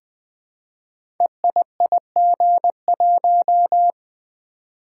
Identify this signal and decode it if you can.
Morse code, heard as a keyed tone, about 20 wpm, 700 Hz. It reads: EIIG1
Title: Code Groups - 5 Letter:Numbers - 20wpm